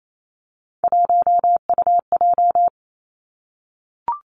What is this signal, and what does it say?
Morse code, heard as a keyed tone, about 28 wpm, 700 Hz. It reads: 1UJ
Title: Three Challenging Letter-Number Combinations 28wpm